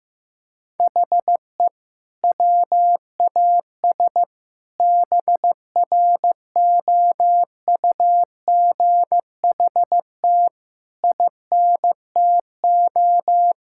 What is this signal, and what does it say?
Morse code, heard as a keyed tone, about 15 wpm, 700 Hz. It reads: HE WAS BROUGHT INTO
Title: Sentences from Top 1600 Words 15wpm